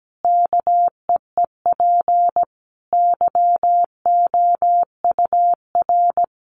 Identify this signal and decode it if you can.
Morse code, heard as a keyed tone, about 17 wpm, 700 Hz. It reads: KEEP YOUR
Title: Sentences from Top 1100 Words 17wpm